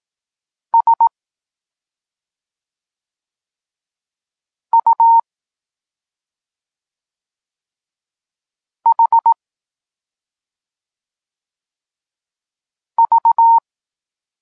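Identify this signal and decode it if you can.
Morse code, heard as a keyed tone, about 18 wpm, 920 Hz. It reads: S U H V
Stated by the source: Polish Morse lesson teaching the letter C, from SP9MOA Niepołomicki Klub Krótkofalowców.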